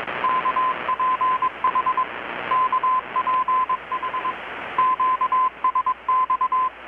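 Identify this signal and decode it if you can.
Morse code, heard as a keyed tone, about 21 wpm, 1030 Hz, with radio noise. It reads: KPH KPH QSX
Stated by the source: Morse from KPH's Night of Nights XX maritime radiotelegraphy event, received in Chiapas, Mexico about 3,800 km away on a Kenwood R-600 with a grounded 10 m long-wire antenna.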